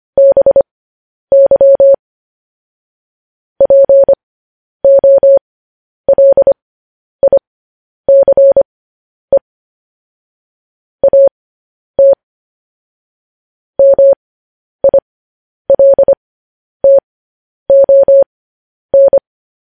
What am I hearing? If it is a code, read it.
Morse code, heard as a keyed tone, about 25 wpm, spaced as for 10 wpm, 550 Hz. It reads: BY POLICE AT MILTON
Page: news headlines in morse code at 10 wpm